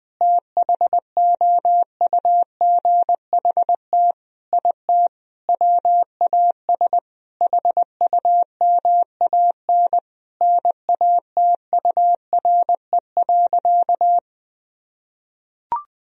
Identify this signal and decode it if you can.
Morse code, heard as a keyed tone, about 21 wpm, 700 Hz. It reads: THOUGHT IT WAS HUMAN NATURE.
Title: Sentences from Top 1000 Words - Review Part 3 - 20wpm